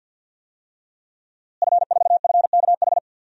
Morse code, heard as a keyed tone, about 52 wpm, 700 Hz. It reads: F4FKH
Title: DXCC-100 Call Signs - 5 Characters - Repeated 3x - 50wpm